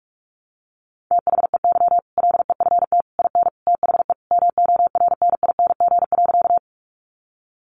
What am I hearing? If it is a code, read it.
Morse code, heard as a keyed tone, about 44 wpm, 700 Hz. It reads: THEY LEFT IN THE MORNING.
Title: Sentences from Top 400 Words 45wpm